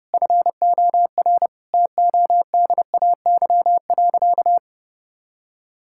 Morse code, heard as a keyed tone, about 30 wpm, 700 Hz. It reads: FOR TODAY.
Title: Sentences from Top 1000 Words - Review Part 5 - 30wpm